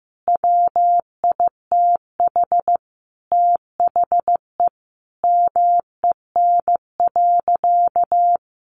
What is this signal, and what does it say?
Morse code, heard as a keyed tone, about 15 wpm, 700 Hz. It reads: WITH THE MEN.
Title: A Sound Of Thunder 15wpm